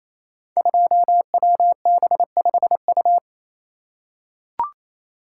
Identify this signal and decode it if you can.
Morse code, heard as a keyed tone, about 28 wpm, 700 Hz. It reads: 2WB5U
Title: Code Groups - 5 Letter:Numbers - 28wpm